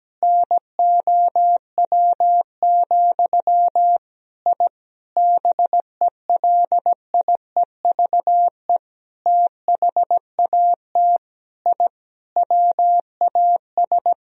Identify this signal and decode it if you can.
Morse code, heard as a keyed tone, about 17 wpm, 700 Hz. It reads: NOW, I BELIEVE THAT I WAS
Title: Sentences from Top 1000 Words - Review Part 1 - 17wpm